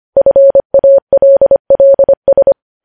Morse code, heard as a keyed tone, about 25 wpm, 550 Hz. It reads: FALLS
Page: news headlines in morse code at 25 wpm